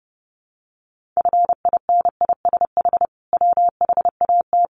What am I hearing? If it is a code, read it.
Morse code, heard as a keyed tone, about 30 wpm, 700 Hz. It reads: FINISH WHAT
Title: Sentences from Top 1200 Words 30wpm